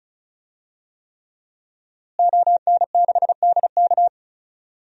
Morse code, heard as a keyed tone, about 34 wpm, 700 Hz. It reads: ON6DK
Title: DXCC-100 Call Signs - 5 Characters - Repeated 3x - 35wpm